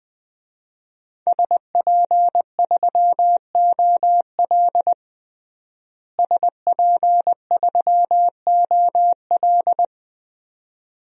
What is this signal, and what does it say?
Morse code, heard as a keyed tone, about 21 wpm, 700 Hz. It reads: SP3OL SP3OL
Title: DXCC-100 Call Signs - 5 Characters - Repeated 2x - 20wpm